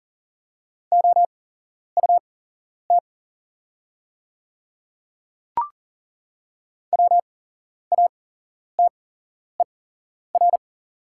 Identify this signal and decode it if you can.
Morse code, heard as a keyed tone, about 40 wpm, spaced as for 11 wpm, 690 Hz. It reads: OUT WATER
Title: 8x Character Spacing - Top 100 Words - 40wpm